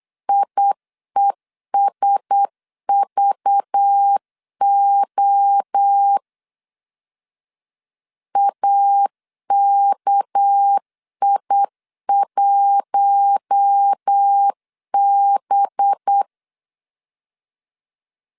Morse code, heard as a keyed tone, about 8 wpm, 800 Hz. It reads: IESVO AKI1B
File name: CW / DET / 1 6 1500C S B 8wpm 5142